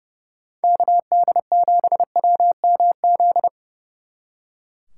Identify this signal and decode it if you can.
Morse code, heard as a keyed tone, about 36 wpm, 700 Hz. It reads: KD7WMZ
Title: Speed Racing - USA 2x3 Call Signs 30wpm